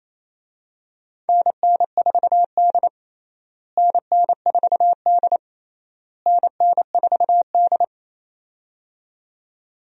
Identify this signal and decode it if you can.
Morse code, heard as a keyed tone, about 28 wpm, 700 Hz. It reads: NN4D NN4D NN4D
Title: USA Call Signs - 4 Characters - Repeated 3x - 28wpm